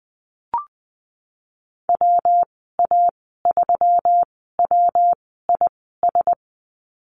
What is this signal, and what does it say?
Morse code, heard as a keyed tone, about 20 wpm, spaced as for 14 wpm, 700 Hz. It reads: WA3WIS
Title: 140 Lesson 23g - Letter f - Call Signs 20wpm